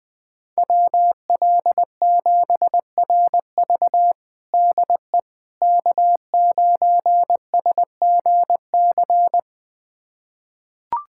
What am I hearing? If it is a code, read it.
Morse code, heard as a keyed tone, about 20 wpm, 700 Hz. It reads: WL7RV DE K9SGC
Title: USA Callsign DE Callsign - 5 Characters - 20wpm